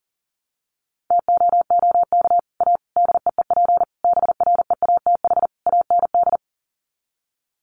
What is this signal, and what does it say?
Morse code, heard as a keyed tone, about 40 wpm, 700 Hz. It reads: TOOK A DEEP BREATH AND
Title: Sets of 5 Words Encore 40wpm